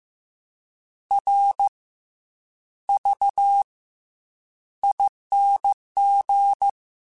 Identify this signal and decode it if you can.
Morse code, heard as a keyed tone, about 15 wpm, spaced as for 8 wpm, 800 Hz. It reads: RVING